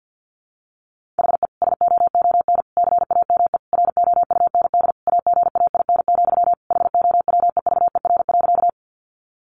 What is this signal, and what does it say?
Morse code, heard as a keyed tone, about 50 wpm, 700 Hz. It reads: HE SOON CAME ROUND AGAIN, HOWEVER.
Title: Sentences from Top 600 Words 50wpm